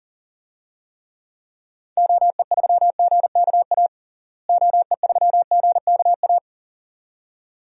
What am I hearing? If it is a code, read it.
Morse code, heard as a keyed tone, about 40 wpm, 700 Hz. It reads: OE3GKA OE3GKA
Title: DXCC-100 Call Signs - 6 Characters - Repeated 2x - 40wpm